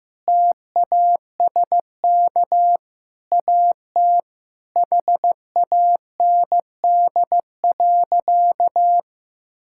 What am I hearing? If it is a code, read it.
Morse code, heard as a keyed tone, about 15 wpm, 700 Hz. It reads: TASK AT HAND.
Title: Sentences from Top 1600 Words 15wpm